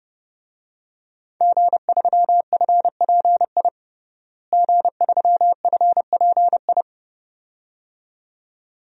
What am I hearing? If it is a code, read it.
Morse code, heard as a keyed tone, about 30 wpm, 700 Hz. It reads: G3FPI G3FPI
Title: DXCC-100 Call Signs - 5 Characters - Repeated 2x - 30wpm